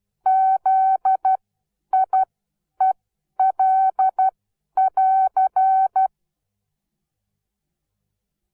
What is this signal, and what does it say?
Morse code, heard as a keyed tone, about 12 wpm, spaced as for 9 wpm, 760 Hz. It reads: ZIEL<AR>
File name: Onlinelehrgang / morsen / DJ4UF Morsen17